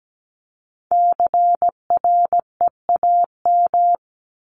Morse code, heard as a keyed tone, about 17 wpm, 700 Hz. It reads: CREAM
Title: Top 1251-1500 Words 17wpm